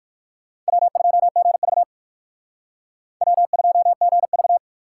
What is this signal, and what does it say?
Morse code, heard as a keyed tone, about 59 wpm, 700 Hz. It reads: W2GV W2GV